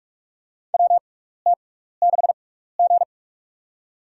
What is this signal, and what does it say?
Morse code, heard as a keyed tone, about 47 wpm, spaced as for 16 wpm, 700 Hz. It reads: WT6G
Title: USA Call Signs - 4 Characters - 6x Character Spacing - 45wpm